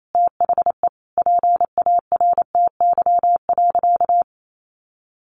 Morse code, heard as a keyed tone, about 28 wpm, 700 Hz. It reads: THE PARTY.
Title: Sentences from Top 1400 Words 28wpm